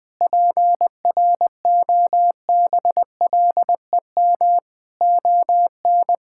Morse code, heard as a keyed tone, about 21 wpm, 700 Hz. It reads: PROBLEM ON
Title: Sentences from Top 1000 Words - Review Part 6 - 20wpm